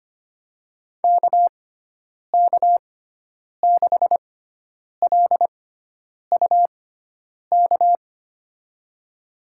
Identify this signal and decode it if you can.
Morse code, heard as a keyed tone, about 25 wpm, spaced as for 9 wpm, 700 Hz. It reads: KK6LUK
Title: USA Call Signs - 6 Characters - 6x Character Spacing - 25wpm